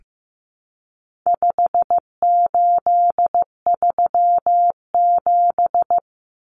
Morse code, heard as a keyed tone, about 15 wpm, 700 Hz. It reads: 5837
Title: Numbers - 4-Digit 15wpm